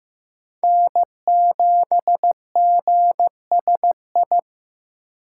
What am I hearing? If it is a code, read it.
Morse code, heard as a keyed tone, about 15 wpm, 700 Hz. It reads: N7GSI
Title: USA Call Signs - 2x Character Spacing - 15wpm